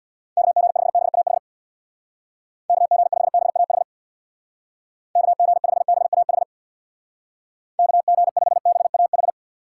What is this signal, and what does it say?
Morse code, heard as a keyed tone, about 62 wpm, 700 Hz. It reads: KG5BAH KG5BAH KG5BAH KG5BAH
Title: Speed Racing - USA 2x3 Call Signs 50wpm